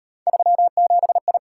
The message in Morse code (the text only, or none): 37I